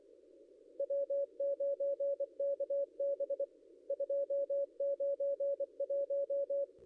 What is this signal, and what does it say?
Morse code, heard as a keyed tone, about 24 wpm, 560 Hz. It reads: W9KB 291
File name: contests / ss-cw-k5tr-op-n5rz -0800